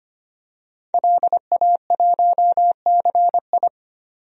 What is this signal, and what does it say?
Morse code, heard as a keyed tone, about 25 wpm, 700 Hz. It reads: LA1CI
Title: DXCC-100 Call Signs - 5 Characters - 25wpm